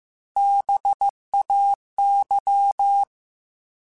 Morse code, heard as a keyed tone, about 15 wpm, 800 Hz. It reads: BAY
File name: fromSMs / FromSMs mp3s / Cities@13